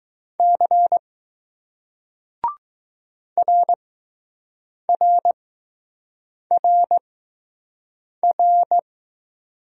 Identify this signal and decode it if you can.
Morse code, heard as a keyed tone, about 20 wpm, spaced as for 6 wpm, 700 Hz. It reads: C RRRR